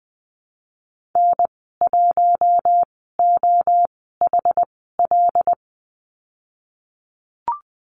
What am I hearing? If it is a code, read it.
Morse code, heard as a keyed tone, about 20 wpm, spaced as for 14 wpm, 700 Hz. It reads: N1OHL
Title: 088 Lesson 15g - Number 5 - Call Signs 20wpm